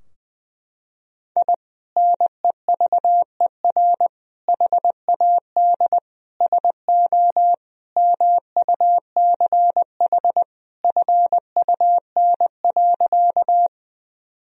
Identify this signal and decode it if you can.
Morse code, heard as a keyed tone, about 21 wpm, 700 Hz. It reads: I NEVER HAD SO MUCH FUN.
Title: Sentences from Top 1000 Words - Review Part 4 - 20wpm